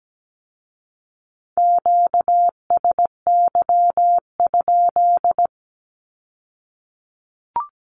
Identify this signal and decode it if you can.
Morse code, heard as a keyed tone, about 17 wpm, 700 Hz. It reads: QSY?
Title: Q-Codes with Spoken Meaning - 17wpm